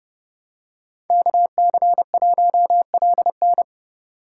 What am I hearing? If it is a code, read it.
Morse code, heard as a keyed tone, about 30 wpm, 700 Hz. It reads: KC1LN